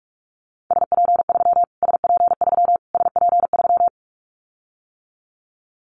Morse code, heard as a keyed tone, about 44 wpm, 700 Hz. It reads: SP3 SP3 SP3
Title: Callsign Prefixes for Top 25 DXCC - Repeated 3x - 45wpm